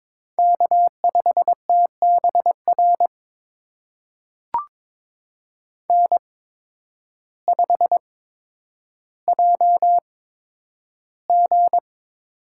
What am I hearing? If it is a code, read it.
Morse code, heard as a keyed tone, about 22 wpm, spaced as for 10 wpm, 700 Hz. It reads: K5TBR N5JG
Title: USA Call Signs - 8x Character Spacing - 22wpm